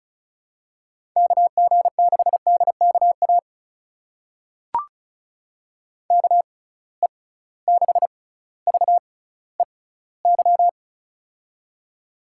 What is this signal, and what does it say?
Morse code, heard as a keyed tone, about 36 wpm, spaced as for 18 wpm, 690 Hz. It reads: KG6DKA KE6VEY
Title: USA Call Signs - 6 Characters - 6x Character Spacing - 35wpm